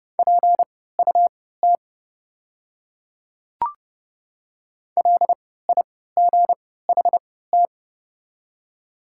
Morse code, heard as a keyed tone, about 32 wpm, spaced as for 17 wpm, 700 Hz. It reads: PUT LIGHT
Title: 3x Character Spacing - Top 300 Words - 30wpm